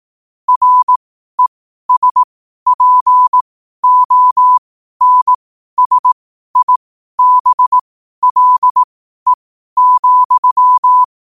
Morse code, heard as a keyed tone, about 18 wpm, spaced as for 12 wpm, 1000 Hz. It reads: RESPONSIBLE,
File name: qotdfiles / 12wpm